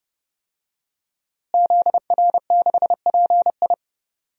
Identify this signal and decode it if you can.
Morse code, heard as a keyed tone, about 30 wpm, 700 Hz. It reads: ZR6PI
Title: DXCC-100 Call Signs - 5 Characters - Repeated 2x - 30wpm